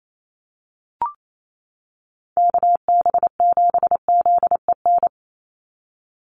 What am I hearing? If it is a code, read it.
Morse code, heard as a keyed tone, about 28 wpm, 700 Hz. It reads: KB7ZEN
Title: USA Call Signs 28wpm